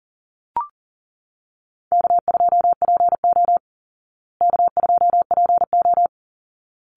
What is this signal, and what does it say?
Morse code, heard as a keyed tone, about 37 wpm, 700 Hz. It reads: K2PO K2PO
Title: Call Signs Repeated 3x - 40wpm